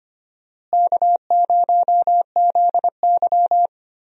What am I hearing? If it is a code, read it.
Morse code, heard as a keyed tone, about 25 wpm, 700 Hz. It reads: K0ZY
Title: USA Call Signs - 4 Characters - Repeated 2x - 25wpm